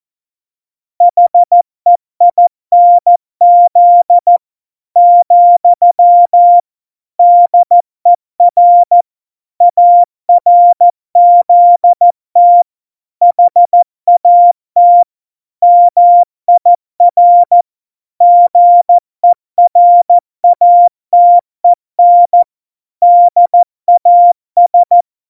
Morse code, heard as a keyed tone, about 14 wpm, 700 Hz. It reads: HEINZ , DER ARZT HAT MIR GERATEN DAS
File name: Morsekurs / Witze / MP3, Witz 10 April 2018, 70 F80